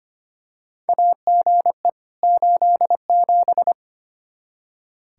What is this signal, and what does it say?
Morse code, heard as a keyed tone, about 25 wpm, 700 Hz. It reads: AGE 87
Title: K1USN SST QSO Practice 25wpm